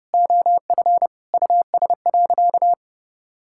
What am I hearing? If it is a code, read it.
Morse code, heard as a keyed tone, about 30 wpm, 700 Hz. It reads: OF US.
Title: Sentences from Top 1000 Words - Review Part 5 - 30wpm